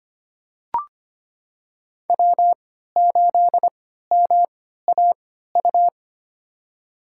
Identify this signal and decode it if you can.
Morse code, heard as a keyed tone, about 25 wpm, spaced as for 14 wpm, 700 Hz. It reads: W8MAU
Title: USA Call Signs - 3x Character Spacing - 25wpm